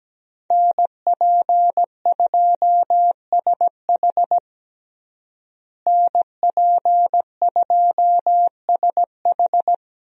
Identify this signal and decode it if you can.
Morse code, heard as a keyed tone, about 17 wpm, 700 Hz. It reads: NP2SH NP2SH
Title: USA Call Signs - 5 Characters - Repeated 2x - 17wpm